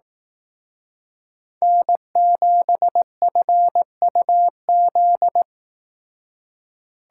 Morse code, heard as a keyed tone, about 18 wpm, 700 Hz. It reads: N7FUZ
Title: Speed Racing - USA Call Signs 15wpm